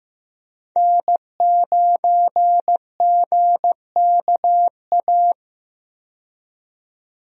Morse code, heard as a keyed tone, about 16 wpm, 700 Hz. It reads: N9GKA
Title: USA Call Signs - 5 Characters - 15wpm